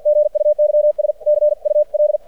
Morse code, heard as a keyed tone, about 50 wpm, 600 Hz. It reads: GUYS CUL
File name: archived / CRSnet